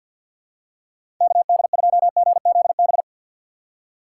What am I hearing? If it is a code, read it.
Morse code, heard as a keyed tone, about 52 wpm, 700 Hz. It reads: KD2GZB
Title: USA Call Signs - 6x Character Spacing - 50wpm